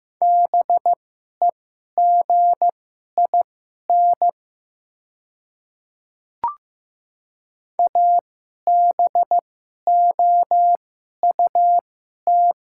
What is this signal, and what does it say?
Morse code, heard as a keyed tone, about 15 wpm, spaced as for 11 wpm, 700 Hz. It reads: BEGIN ABOUT
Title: 2x Character Spacing - Top 300 Words - 15wpm